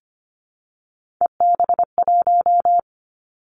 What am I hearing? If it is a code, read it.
Morse code, heard as a keyed tone, about 25 wpm, 700 Hz. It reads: EB1
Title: Three Challenging Letter-Number Combinations 25wpm